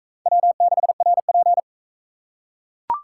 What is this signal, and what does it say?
Morse code, heard as a keyed tone, about 40 wpm, 700 Hz. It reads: W6RP